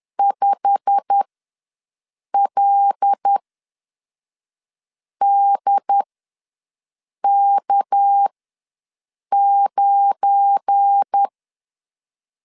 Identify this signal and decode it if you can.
Morse code, heard as a keyed tone, about 10 wpm, spaced as for 5 wpm, 800 Hz. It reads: 5LDK9
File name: LEZIONI / 9 2 ws 10wpm 4130